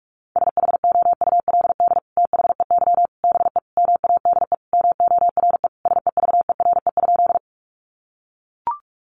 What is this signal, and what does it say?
Morse code, heard as a keyed tone, about 44 wpm, 700 Hz. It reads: SHOULD THEY BE MADE MORE SEVERE?
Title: Sentences from Top 1400 Words 45wpm